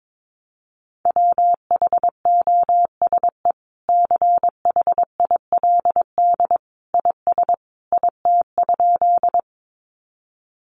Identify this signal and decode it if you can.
Morse code, heard as a keyed tone, about 22 wpm, 700 Hz. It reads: WHOSE CHILD IS IT?
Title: Sentences from Top 1200 Words 22wpm